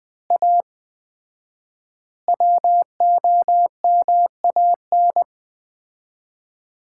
Morse code, heard as a keyed tone, about 20 wpm, 700 Hz. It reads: A WOMAN
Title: 4x Word-Spacing - Sets of 5 Words - 20wpm